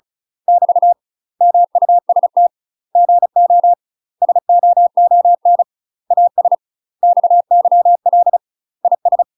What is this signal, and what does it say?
Morse code, heard as a keyed tone, about 34 wpm, 700 Hz. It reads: <BT> MUST GO SOON AS XYL IS
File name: UK MASTER QSOs TRAINING / 35WPM / UK MASTER QSOs 0039